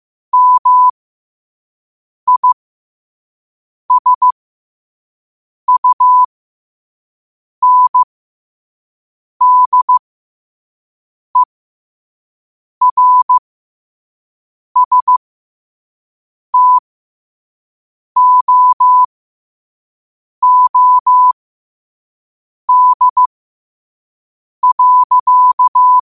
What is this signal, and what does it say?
Morse code, heard as a keyed tone, about 15 wpm, spaced as for 5 wpm, 1000 Hz. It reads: MISUNDERSTOOD.
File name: qotdfiles / 5wpm